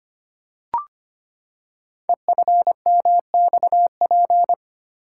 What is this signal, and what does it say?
Morse code, heard as a keyed tone, about 25 wpm, 700 Hz. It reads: EFMXP
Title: Code Groups - 5 Letters - 25wpm